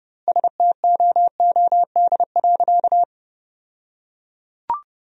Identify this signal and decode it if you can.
Morse code, heard as a keyed tone, about 30 wpm, 700 Hz. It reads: STOOD.
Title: 2x Word-Spacing - Sentences from Top 500 Words - 30wpm